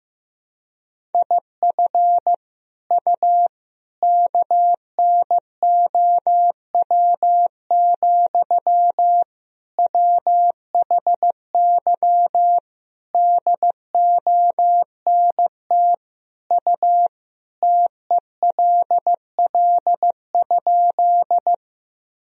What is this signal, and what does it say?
Morse code, heard as a keyed tone, about 15 wpm, 700 Hz. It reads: IF U KNOW, WHY DONT U TELL?
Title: Sentences from Top 300 Words and Abbreviations 15wpm